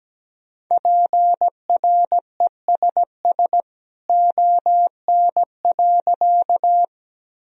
Morse code, heard as a keyed tone, about 17 wpm, 700 Hz. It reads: PRESS ON.
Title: Sentences from Top 1000 Words - Review Part 1 - 17wpm